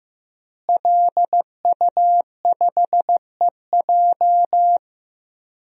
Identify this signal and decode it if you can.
Morse code, heard as a keyed tone, about 15 wpm, 700 Hz. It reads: LU5EJ
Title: DXCC-100 Call Signs - 5 Characters - Repeated 2x - 15wpm